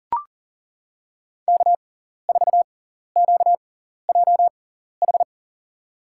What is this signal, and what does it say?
Morse code, heard as a keyed tone, about 40 wpm, spaced as for 14 wpm, 700 Hz. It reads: K4QJH
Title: USA Call Signs - 5 Characters - 6x Character Spacing - 40wpm